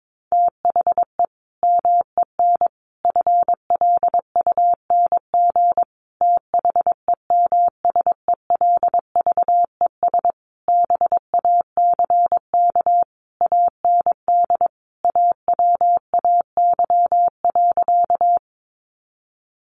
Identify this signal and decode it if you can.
Morse code, heard as a keyed tone, about 22 wpm, 700 Hz. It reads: THE MEN FLUNG THEMSELVES BACK AND AWAY.
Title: A Sound Of Thunder 22wpm